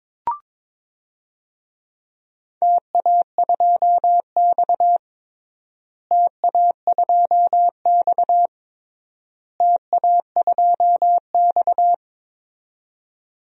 Morse code, heard as a keyed tone, about 22 wpm, 700 Hz. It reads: TA2X TA2X TA2X
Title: DXCC-100 Call Signs - 4 Characters - Repeated 3x - 22wpm